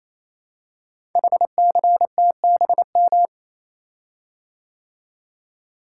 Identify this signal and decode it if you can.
Morse code, heard as a keyed tone, about 27 wpm, 700 Hz. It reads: HCTBM
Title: Code Groups - 5 Letters - 28wpm